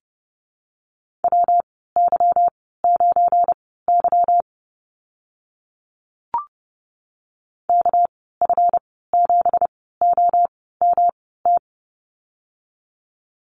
Morse code, heard as a keyed tone, about 30 wpm, spaced as for 17 wpm, 700 Hz. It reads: WY9Y KF7OMT
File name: Lesson 37g - Letter x - Call Signs 30wpm